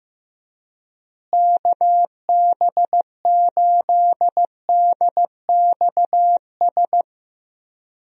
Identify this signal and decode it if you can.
Morse code, heard as a keyed tone, about 15 wpm, 700 Hz. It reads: KB8DXS
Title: USA Call Signs - 6 Characters - Repeated 2x - 15wpm